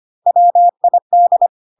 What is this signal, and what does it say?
Morse code, heard as a keyed tone, about 25 wpm, 700 Hz. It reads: WID
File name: cw / CW common abbreviations alphabetical repeated 3 times 25 25wpm ik1hge